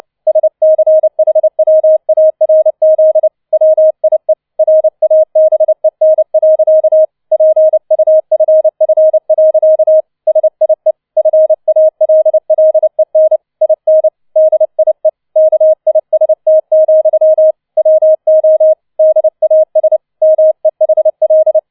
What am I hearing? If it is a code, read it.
Morse code, heard as a keyed tone, about 29 wpm, 630 Hz. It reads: SCHWARZ WIE RABEN. PUFF. SIE FALLEN IN DIE KIST, WO DAS MEHL
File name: 006 Sechster Streich 30wpm